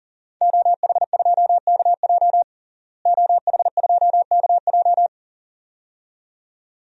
Morse code, heard as a keyed tone, about 40 wpm, 700 Hz. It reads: OH2KJ OH2KJ
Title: DXCC-100 Call Signs - 5 Characters - Repeated 2x - 40wpm